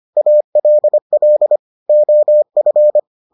Morse code, heard as a keyed tone, about 25 wpm, 600 Hz. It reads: ALL OF